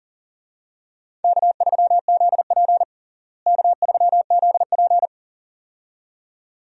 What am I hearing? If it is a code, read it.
Morse code, heard as a keyed tone, about 37 wpm, 700 Hz. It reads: K3ZP K3ZP
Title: USA Call Signs - 4 Characters - Repeated 2x - 40wpm